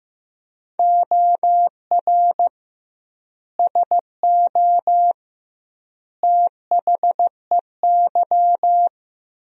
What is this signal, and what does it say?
Morse code, heard as a keyed tone, about 15 wpm, 700 Hz. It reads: OR SO THEY
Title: 2x Word-Spacing - Sentences from Top 100 Words - 15wpm